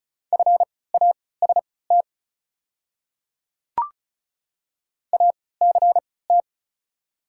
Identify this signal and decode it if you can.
Morse code, heard as a keyed tone, about 34 wpm, spaced as for 20 wpm, 700 Hz. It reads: FAST ACT
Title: 3x Character Spacing - Top 500 Words - 35wpm